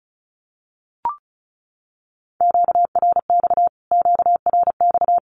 Morse code, heard as a keyed tone, about 34 wpm, 700 Hz. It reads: QRX QRX
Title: Q-Codes Repeated 3x 35wpm